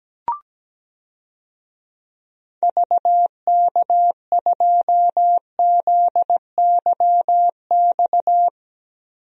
Morse code, heard as a keyed tone, about 17 wpm, 700 Hz. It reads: VK2ZYX
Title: DXCC-100 Call Signs - 6 Characters - Repeated 3x - 17wpm